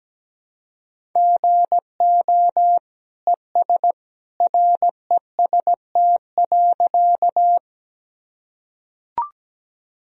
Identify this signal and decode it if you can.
Morse code, heard as a keyed tone, about 17 wpm, 700 Hz. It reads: GO ES REST.
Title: Sentences from Top 500 Words and Abbreviations 17wpm